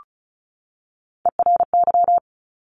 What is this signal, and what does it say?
Morse code, heard as a keyed tone, about 36 wpm, 700 Hz. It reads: ERY
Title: Top Three Letter Combinations in Words 35wpm